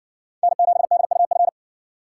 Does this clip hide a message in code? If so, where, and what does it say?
Yes: Morse code from 0.4 to 1.5 s, heard as a keyed tone, about 57 wpm, 700 Hz. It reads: N7DUF